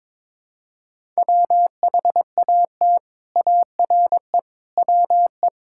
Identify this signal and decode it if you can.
Morse code, heard as a keyed tone, about 22 wpm, 700 Hz. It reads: WHAT ARE WE
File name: practice / 22wpm / Sets of 6 Words - 22wpm